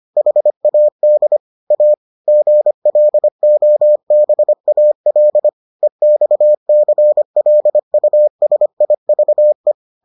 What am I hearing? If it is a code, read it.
Morse code, heard as a keyed tone, about 25 wpm, 600 Hz. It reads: HAD A GLOBAL EXCLUSIVE